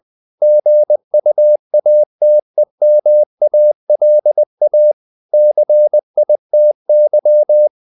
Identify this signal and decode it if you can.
Morse code, heard as a keyed tone, about 20 wpm, 600 Hz. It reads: GUATEMALA CITY
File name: CW / Practice / NationalCapitals20